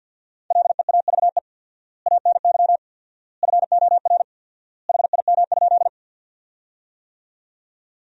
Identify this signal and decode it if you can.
Morse code, heard as a keyed tone, about 53 wpm, 700 Hz. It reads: LEAVE ANY FOR HIM?
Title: 4x Word-Spacing - Sentences from Top 400 Words - 50wpm